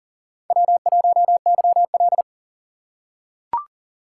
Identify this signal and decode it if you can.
Morse code, heard as a keyed tone, about 40 wpm, 700 Hz. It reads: W1YL
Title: USA Call Signs - 4 Characters - 3x Character Spacing - 40wpm